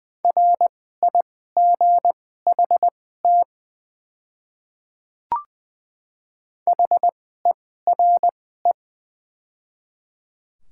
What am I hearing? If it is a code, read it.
Morse code, heard as a keyed tone, about 20 wpm, spaced as for 15 wpm, 700 Hz. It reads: RIGHT HERE
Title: 2x Character Spacing - Top 200 Words - 20wpm